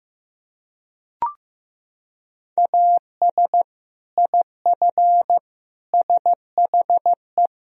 Morse code, heard as a keyed tone, about 15 wpm, 700 Hz. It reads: AS IF SHE
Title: Sentences from Top 1000 Words - Review Part 1 - 15wpm